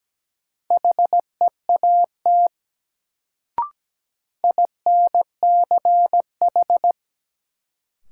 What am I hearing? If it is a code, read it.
Morse code, heard as a keyed tone, about 17 wpm, 700 Hz. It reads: HEAT INCH
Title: Sentences from Top 1000 Words - Review Part 3 - New Word Drill - 17wpm